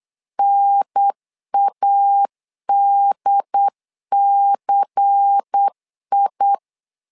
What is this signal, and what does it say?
Morse code, heard as a keyed tone, about 8 wpm, 800 Hz. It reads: NADCI
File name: LEZIONI / 5 LBVC / 5 LBVC 8wpm 2935